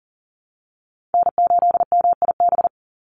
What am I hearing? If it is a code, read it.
Morse code, heard as a keyed tone, about 40 wpm, 700 Hz. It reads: N8MIB
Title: USA Call Signs 40wpm